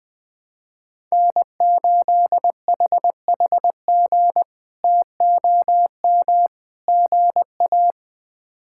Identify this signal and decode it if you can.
Morse code, heard as a keyed tone, about 20 wpm, 700 Hz. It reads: N8HHG TOM GA
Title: K1USN SST QSO Practice - 20wpm